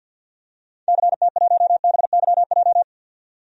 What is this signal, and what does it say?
Morse code, heard as a keyed tone, about 50 wpm, 700 Hz. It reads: CT1BYJ
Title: DXCC-100 Call Signs - 6 Characters - 50wpm